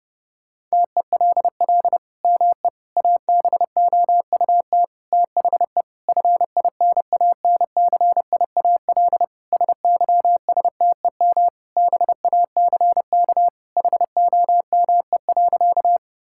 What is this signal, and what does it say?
Morse code, heard as a keyed tone, about 30 wpm, 700 Hz. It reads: TELL ME ABOUT THE FINANCIAL SYSTEM BACK HOME.
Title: Sentences from Top 1000 Words - Review Part 5 - 30wpm